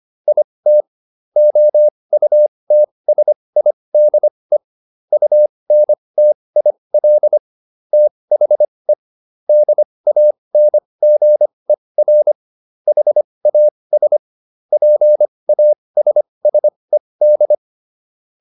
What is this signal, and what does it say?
Morse code, heard as a keyed tone, about 25 wpm, spaced as for 20 wpm, 600 Hz. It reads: IT OUTSIDE UNTIL THE DANGER HAS PASSED